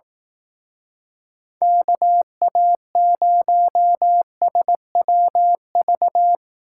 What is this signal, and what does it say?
Morse code, heard as a keyed tone, about 18 wpm, 700 Hz. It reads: KA0SWV
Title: Speed Racing - USA 2x3 Call Signs 15wpm